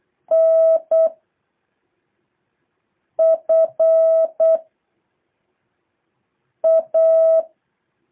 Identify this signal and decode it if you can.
Morse code, heard as a keyed tone, about 8 wpm, 660 Hz. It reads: N F A
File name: jh8cbh / 2023cw